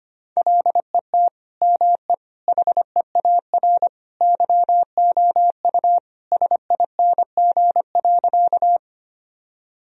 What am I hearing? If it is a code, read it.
Morse code, heard as a keyed tone, about 25 wpm, 700 Hz. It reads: LET ME HEAR YOU SING.
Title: Sentences from Top 1000 Words - Review Part 2 - 25wpm